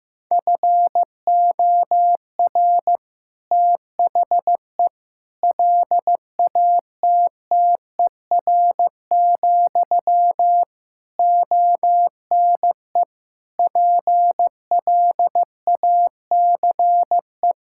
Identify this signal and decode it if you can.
Morse code, heard as a keyed tone, about 16 wpm, 700 Hz. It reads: FOR THE LATTER, ONE PLACE
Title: Sentences from Top 1600 Words 15wpm